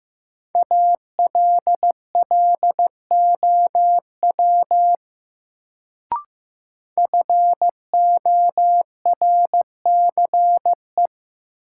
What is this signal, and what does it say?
Morse code, heard as a keyed tone, about 16 wpm, 700 Hz. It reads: ALLOW FORCE
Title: Sentences from Top 700 Words - New Word Drill - 15wpm